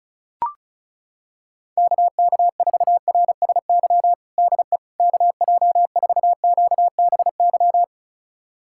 Morse code, heard as a keyed tone, about 34 wpm, 700 Hz. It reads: KK4RSY DE KJ4QBY
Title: USA Callsign DE Callsign - 6 Characters - 35wpm